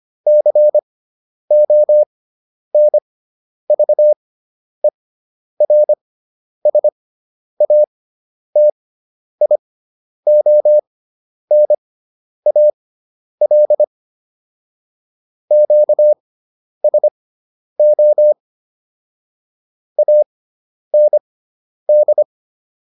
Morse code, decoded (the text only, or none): CONVERSATIONAL QSO AND